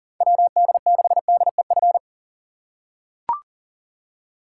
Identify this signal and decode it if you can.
Morse code, heard as a keyed tone, about 40 wpm, 700 Hz. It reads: WD6DEF
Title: USA Call Signs - 6 Characters - 6x Character Spacing - 40wpm